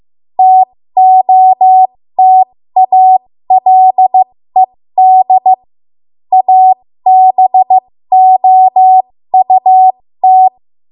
Morse code, heard as a keyed tone, about 15 wpm, spaced as for 13 wpm, 750 Hz. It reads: TOTALED ABOUT
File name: Morse / 13WPM